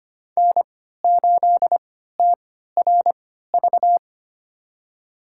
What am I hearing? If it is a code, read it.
Morse code, heard as a keyed tone, about 25 wpm, spaced as for 14 wpm, 700 Hz. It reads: N8TRV
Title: USA Call Signs - 5 Characters - 3x Character Spacing - 25wpm